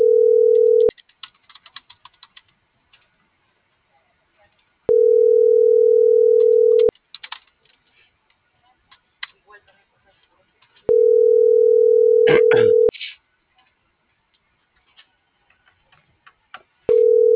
Ringback tone (North America), 440 + 480 Hz.